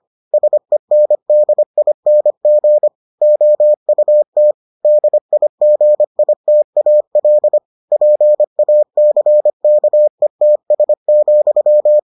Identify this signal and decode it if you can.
Morse code, heard as a keyed tone, about 25 wpm, 600 Hz. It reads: SENDING OUT DIGITAL PACKETS,